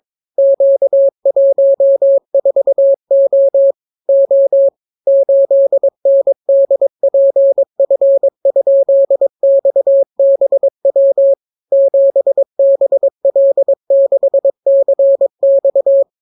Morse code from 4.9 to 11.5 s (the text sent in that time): 8NDPF?XBW